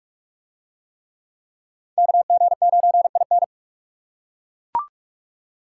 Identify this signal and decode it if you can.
Morse code, heard as a keyed tone, about 41 wpm, 700 Hz. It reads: KG9IN